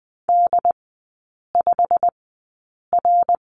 Morse code, heard as a keyed tone, about 20 wpm, spaced as for 8 wpm, 700 Hz. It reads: D5R
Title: 087 Lesson 15e - Number 5 - Three Characters 20wpm